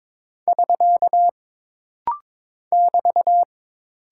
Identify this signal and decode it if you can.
Morse code, heard as a keyed tone, about 22 wpm, 700 Hz. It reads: <SK> <BT>
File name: practice / 22wpm / Single Punctuation-Prosign - Rapid-Fire - 22wpm